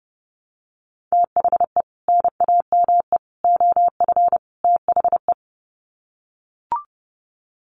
Morse code, heard as a keyed tone, about 30 wpm, 700 Hz. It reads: THE NAME OF THE
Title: Sets of 4 Words 30wpm